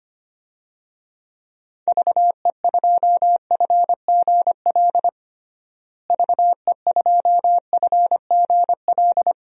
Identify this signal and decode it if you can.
Morse code, heard as a keyed tone, about 25 wpm, 700 Hz. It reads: VE2FGL VE2FGL
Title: DXCC-100 Call Signs - 6 Characters - Repeated 3x - 25wpm